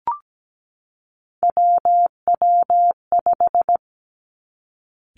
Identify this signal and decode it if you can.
Morse code, heard as a keyed tone, about 17 wpm, 700 Hz. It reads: WW5
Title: Three Challenging Letter-Number Combinations 17wpm